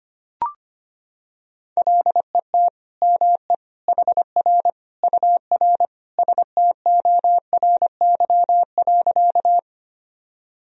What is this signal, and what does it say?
Morse code, heard as a keyed tone, about 25 wpm, 700 Hz. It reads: LET ME HR UR STORY.
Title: Sentences from Top 300 Words and Abbreviations 25wpm